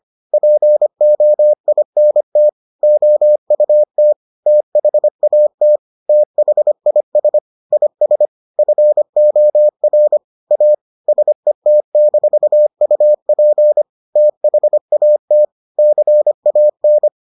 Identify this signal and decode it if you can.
Morse code, heard as a keyed tone, about 25 wpm, 600 Hz. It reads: POINT OUT THAT THIS IS FOR A SET-UP THAT CAN